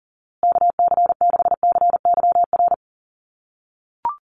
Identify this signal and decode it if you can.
Morse code, heard as a keyed tone, about 37 wpm, 700 Hz. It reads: KC6CYR
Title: USA Call Signs Repeated 2x 40wpm